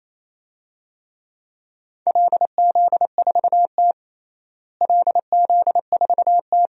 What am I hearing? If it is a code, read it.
Morse code, heard as a keyed tone, about 28 wpm, 700 Hz. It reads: LZ4T LZ4T
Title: DXCC-100 Call Signs - 4 Characters - Repeated 2x - 28wpm